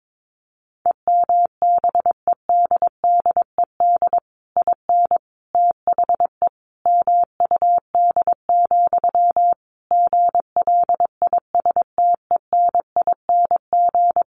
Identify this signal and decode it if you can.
Morse code, heard as a keyed tone, about 22 wpm, 700 Hz. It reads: EMBEDDED IN THE MUD, GLISTENING
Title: A Sound Of Thunder 22wpm